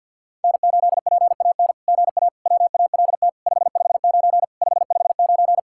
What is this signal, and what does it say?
Morse code, heard as a keyed tone, about 48 wpm, 700 Hz. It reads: N8PAN GA WALT 559 559
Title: SOTA QSO Practice 50wpm